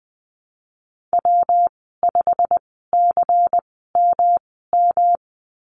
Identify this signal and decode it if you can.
Morse code, heard as a keyed tone, about 20 wpm, spaced as for 15 wpm, 700 Hz. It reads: W5CMM
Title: 167 Lesson 27g - Number 7 - Call Signs 20wpm